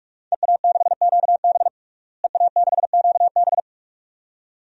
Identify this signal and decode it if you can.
Morse code, heard as a keyed tone, about 46 wpm, 700 Hz. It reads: EA6QB EA6QB
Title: DXCC-100 Call Signs - 5 Characters - Repeated 2x - 45wpm